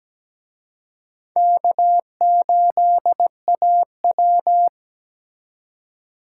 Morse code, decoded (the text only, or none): K8AW